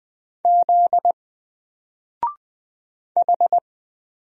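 Morse code, heard as a keyed tone, about 20 wpm, 700 Hz. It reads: Z H
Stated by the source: Single letter and number Morse drill at 20 wpm, from N4LCD.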